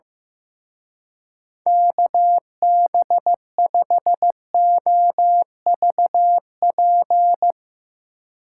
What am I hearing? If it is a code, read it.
Morse code, heard as a keyed tone, about 16 wpm, 700 Hz. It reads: KB5OVP
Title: Speed Racing - USA 2x3 Call Signs 15wpm